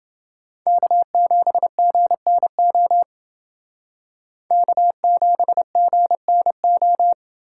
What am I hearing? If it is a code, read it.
Morse code, heard as a keyed tone, about 28 wpm, 700 Hz. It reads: K7GNO K7GNO